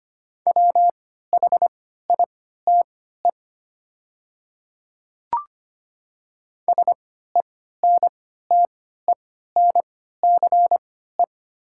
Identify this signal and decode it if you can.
Morse code, heard as a keyed tone, about 25 wpm, spaced as for 14 wpm, 700 Hz. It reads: WHITE SENTENCE
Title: 3x Character Spacing - Top 500 Words - 25wpm